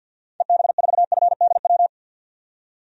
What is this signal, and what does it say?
Morse code, heard as a keyed tone, about 51 wpm, 700 Hz. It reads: EB4FDW
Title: DXCC-100 Call Signs - 6 Characters - 50wpm